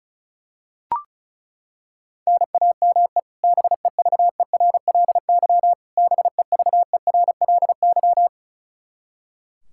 Morse code, heard as a keyed tone, about 36 wpm, 700 Hz. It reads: NAME BEVERLY BEVERLY
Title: Practice QSO - Exchange 1 - Beginning - 35wpm